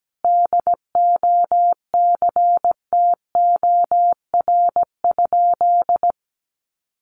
Morse code, heard as a keyed tone, about 17 wpm, 700 Hz. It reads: DOCTOR?
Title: Sentences from Top 1500 Words 17wpm